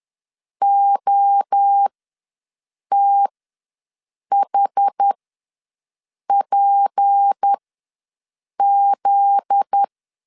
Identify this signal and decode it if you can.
Morse code, heard as a keyed tone, about 10 wpm, spaced as for 6 wpm, 800 Hz. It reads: OTHPZ
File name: LEZIONI / 9 286 ws 10wpm 4032